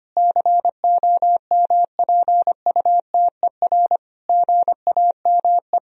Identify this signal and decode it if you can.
Morse code, heard as a keyed tone, about 25 wpm, 700 Hz. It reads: COMPUTER GAME
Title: Sentences from Top 1000 Words - Review Part 3 - 25wpm